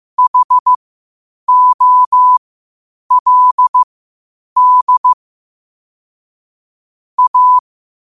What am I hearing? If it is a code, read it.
Morse code, heard as a keyed tone, about 15 wpm, spaced as for 8 wpm, 1000 Hz. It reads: HOLD A
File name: qotdfiles / 8wpm